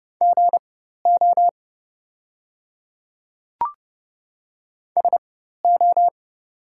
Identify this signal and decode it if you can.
Morse code, heard as a keyed tone, about 30 wpm, spaced as for 14 wpm, 700 Hz. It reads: GO SO
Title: 4x Character Spacing - Top 100 Words - 30wpm